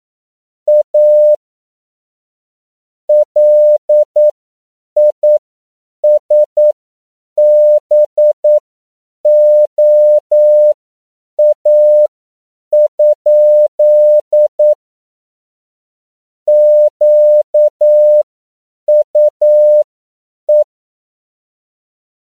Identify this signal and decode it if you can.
Morse code, heard as a keyed tone, about 9 wpm, spaced as for 7 wpm, 600 Hz. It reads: A LISBOA? QUE
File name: CW / 10PPM